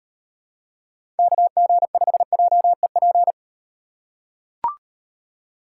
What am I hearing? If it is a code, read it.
Morse code, heard as a keyed tone, about 38 wpm, 700 Hz. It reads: KG5JEP